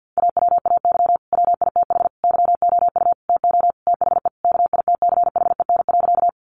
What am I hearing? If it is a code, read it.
Morse code, heard as a keyed tone, about 50 wpm, 700 Hz. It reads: AWAY WITH YOU TO THE KITCHEN.
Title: Sentences from Top 1400 Words 50wpm